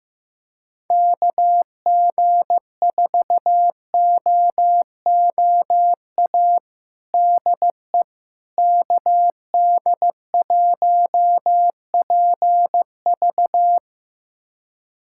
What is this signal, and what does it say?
Morse code, heard as a keyed tone, about 15 wpm, 700 Hz. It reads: KG4OOA DE KD1PV
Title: USA Callsign DE Callsign 15wpm